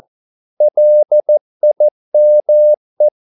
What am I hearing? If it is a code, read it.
Morse code, heard as a keyed tone, about 14 wpm, 600 Hz. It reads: LIME
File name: Korte ord DK 14wpm